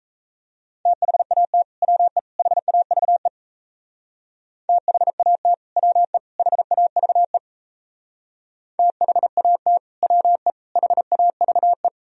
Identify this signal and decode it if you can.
Morse code, heard as a keyed tone, about 36 wpm, 700 Hz. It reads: THAT WE HAVE THAT WE HAVE THAT WE HAVE